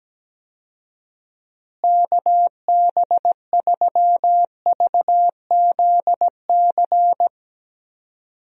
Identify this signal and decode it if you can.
Morse code, heard as a keyed tone, about 17 wpm, 700 Hz. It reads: KB3VZC
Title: USA Call Signs - 6 Characters - Repeated 3x - 17wpm